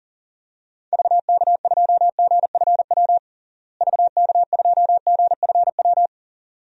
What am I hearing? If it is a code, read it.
Morse code, heard as a keyed tone, about 40 wpm, 700 Hz. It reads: VK2GFW VK2GFW
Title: DXCC-100 Call Signs - 6 Characters - Repeated 2x - 40wpm